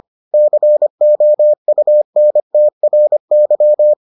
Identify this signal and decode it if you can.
Morse code, heard as a keyed tone, about 25 wpm, 600 Hz. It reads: COUNTRY